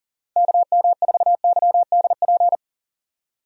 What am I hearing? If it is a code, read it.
Morse code, heard as a keyed tone, about 37 wpm, 690 Hz. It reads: KM4YDP
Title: USA Call Signs - 4x Character Spacing - 40wpm